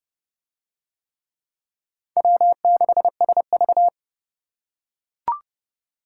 Morse code, heard as a keyed tone, about 29 wpm, 700 Hz. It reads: W6SV